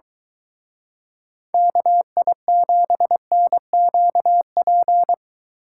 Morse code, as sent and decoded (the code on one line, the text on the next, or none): -.- .. --... -. --.- .--.
KI7NQP